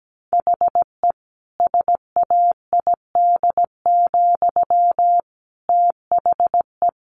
Morse code, heard as a keyed tone, about 17 wpm, 700 Hz. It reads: HE SAID, THE
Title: Sentences from Top 1200 Words 17wpm